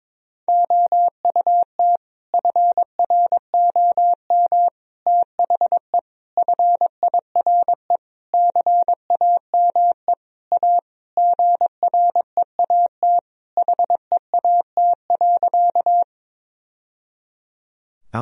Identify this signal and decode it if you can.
Morse code, heard as a keyed tone, about 22 wpm, 700 Hz. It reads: OUT FROM THE FIRE CAME A GREAT HEAT.
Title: Sentences from Top 1000 Words - Review Part 3 - 22wpm